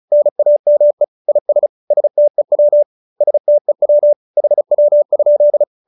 Morse code, heard as a keyed tone, about 34 wpm, 590 Hz. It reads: NAME IS STEW STEW HW?